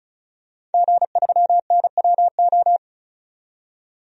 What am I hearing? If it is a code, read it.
Morse code, heard as a keyed tone, about 34 wpm, 700 Hz. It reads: G3NWO
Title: DXCC-100 Call Signs - 5 Characters - 35wpm